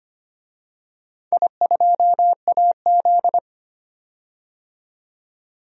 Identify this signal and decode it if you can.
Morse code, heard as a keyed tone, about 25 wpm, 700 Hz. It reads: I2AZ
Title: DXCC-100 Call Signs - 4 Characters - 25wpm